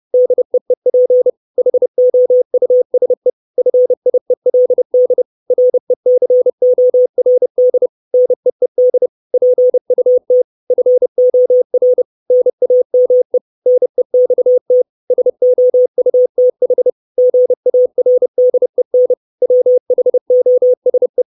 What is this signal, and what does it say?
Morse code, heard as a keyed tone, about 30 wpm, 500 Hz. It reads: DEEP HOUSE FIELD RECORD NEED PUT FOR NAME NEXT SOUTH GARDEN WHOSE
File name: 632 Words 30WPM